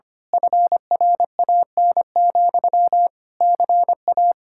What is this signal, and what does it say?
Morse code, heard as a keyed tone, about 26 wpm, 700 Hz. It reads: FRAN, CA
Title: Speed-Racing Common QSO Phrases - 25wpm